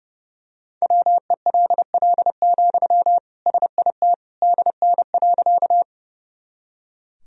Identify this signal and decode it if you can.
Morse code, heard as a keyed tone, about 30 wpm, 700 Hz. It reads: WELL, SIT DN.
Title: Sentences from Top 500 Words and Abbreviations 30wpm